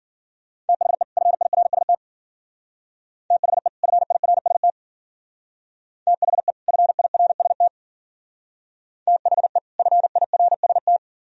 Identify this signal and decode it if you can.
Morse code, heard as a keyed tone, about 47 wpm, 690 Hz. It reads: THE FIRST THE FIRST THE FIRST THE FIRST
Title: Speed-Racing - Sets of 2 Words 40wpm